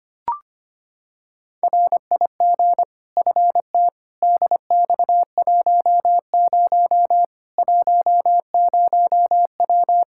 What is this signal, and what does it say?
Morse code, heard as a keyed tone, about 25 wpm, 700 Hz. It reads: RIG FT DX10 10W
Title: Practice QSO - Exchange 2 - Info - 25wpm